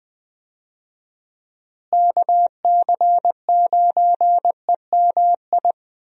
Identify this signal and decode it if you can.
Morse code, heard as a keyed tone, about 20 wpm, 700 Hz. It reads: KC9EMI
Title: USA Call Signs - 6 Characters - Repeated 2x - 20wpm